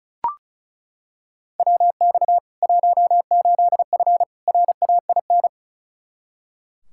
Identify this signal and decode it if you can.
Morse code, heard as a keyed tone, about 36 wpm, 700 Hz. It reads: WX 18F RAIN
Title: Practice QSO - Exchange 2 - Info - 35wpm